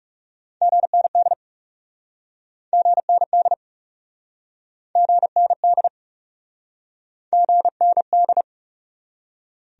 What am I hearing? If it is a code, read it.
Morse code, heard as a keyed tone, about 37 wpm, 700 Hz. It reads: GND GND GND GND
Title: Speed-Racing Abbreviations - 30wpm